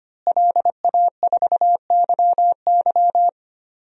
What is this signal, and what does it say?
Morse code, heard as a keyed tone, about 25 wpm, 700 Hz. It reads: LA4YY
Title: DXCC-100 Call Signs - 5 Characters - Repeated 2x - 25wpm